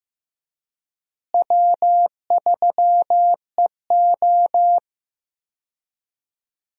Morse code, heard as a keyed tone, about 16 wpm, 700 Hz. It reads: W3EO
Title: USA Call Signs - 4 Characters - Repeated 3x - 15wpm